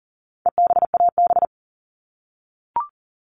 Morse code, heard as a keyed tone, about 40 wpm, 700 Hz. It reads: EBAB
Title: All Cut Numbers - 4-Digit 40wpm